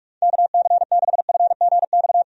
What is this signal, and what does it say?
Morse code, heard as a keyed tone, about 46 wpm, 700 Hz. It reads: KC6FGX
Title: USA Call Signs - 6 Characters - Repeated 3x - 45wpm